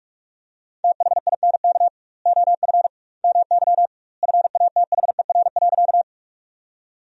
Morse code, heard as a keyed tone, about 46 wpm, 700 Hz. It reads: THINK OF MY FATHER.
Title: 2x Word-Spacing - Sentences from Top 300 Words - 45wpm